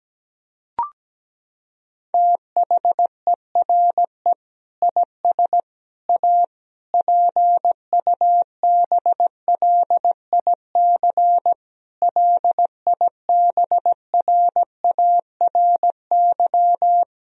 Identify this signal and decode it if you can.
Morse code, heard as a keyed tone, about 17 wpm, 700 Hz. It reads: THERE IS A PUBLIC LIBRARY
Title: Sentences from Top 1000 Words - Review Part 4 - 17wpm